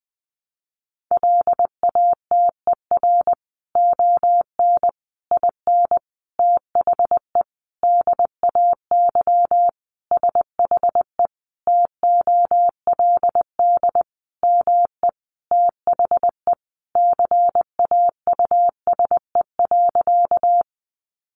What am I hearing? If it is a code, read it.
Morse code, heard as a keyed tone, about 20 wpm, 700 Hz. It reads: LATER ON IN THE DAY SHE TOLD ME THE CAUSE.
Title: Sentences from Top 700 Words 20wpm